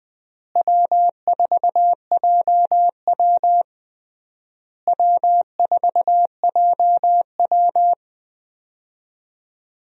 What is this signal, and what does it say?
Morse code, heard as a keyed tone, about 21 wpm, 700 Hz. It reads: W4JW W4JW
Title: USA Call Signs - 4 Characters - Repeated 3x - 20wpm